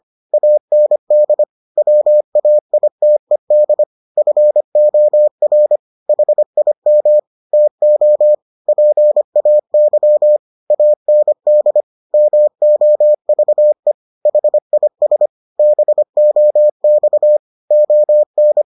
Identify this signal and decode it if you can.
Morse code, heard as a keyed tone, about 25 wpm, 600 Hz. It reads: AND WAITED FOR HIM TO PAY AND MOVE HIS BOX ON